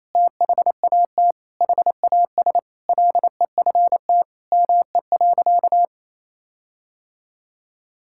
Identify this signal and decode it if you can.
Morse code, heard as a keyed tone, about 28 wpm, 700 Hz. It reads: THAT HAS LEFT ME.
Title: Sentences from Top 1600 Words 28wpm